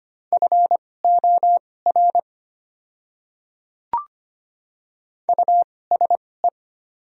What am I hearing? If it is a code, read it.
Morse code, heard as a keyed tone, about 25 wpm, spaced as for 18 wpm, 690 Hz. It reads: FOR USE
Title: 2x Character Spacing - Top 100 Words - 25wpm